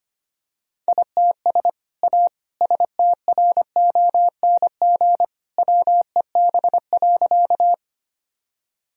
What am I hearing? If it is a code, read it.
Morse code, heard as a keyed tone, about 25 wpm, 690 Hz. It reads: ITS A STRONG WEB.
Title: Sentences from Top 1000 Words - Review Part 4 - 25wpm